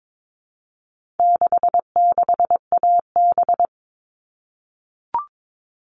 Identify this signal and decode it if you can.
Morse code, heard as a keyed tone, about 22 wpm, 700 Hz. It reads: 66AB
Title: All Cut Numbers - 4-Digit 22wpm